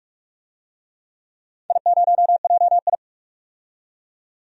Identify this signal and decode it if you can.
Morse code, heard as a keyed tone, about 46 wpm, 700 Hz. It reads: I0JI
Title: DXCC-100 Call Signs - 4 Characters - 45wpm